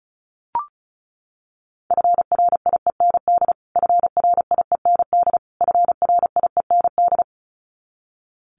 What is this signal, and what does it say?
Morse code, heard as a keyed tone, about 34 wpm, 700 Hz. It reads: FRIEND FRIEND FRIEND
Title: Top 500 Words Repeated 35wpm